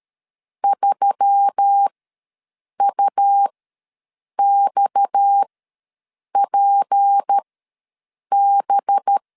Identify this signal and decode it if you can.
Morse code, heard as a keyed tone, about 13 wpm, spaced as for 7 wpm, 800 Hz. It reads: 3UXPB
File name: LEZIONI / 6 PKX0 / 6 PKX0 ws 12wpm 3448